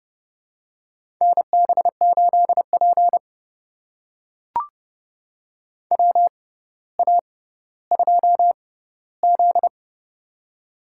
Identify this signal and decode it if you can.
Morse code, heard as a keyed tone, about 30 wpm, spaced as for 15 wpm, 700 Hz. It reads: NB8P WA2Z
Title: USA Call Signs - 4 Characters - 6x Character Spacing - 30wpm